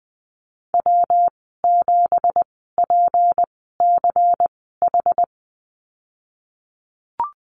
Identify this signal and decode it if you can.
Morse code, heard as a keyed tone, about 20 wpm, spaced as for 15 wpm, 700 Hz. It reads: W7PCH
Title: 179 Lesson 29g - Slash - Call Signs 20wpm